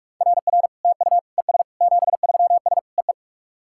Morse code, heard as a keyed tone, about 44 wpm, 700 Hz. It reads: RR TU ES 73S EE
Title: SOTA QSO Practice 45wpm